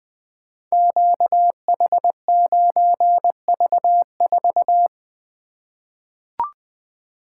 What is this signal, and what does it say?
Morse code, heard as a keyed tone, about 20 wpm, 700 Hz. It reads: QH9V4
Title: Code Groups - 5 Letter:Numbers - 20wpm